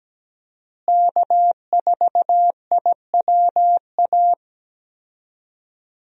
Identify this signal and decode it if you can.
Morse code, heard as a keyed tone, about 17 wpm, 700 Hz. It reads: K4IWA
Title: USA Call Signs - 5 Characters - Repeated 3x - 17wpm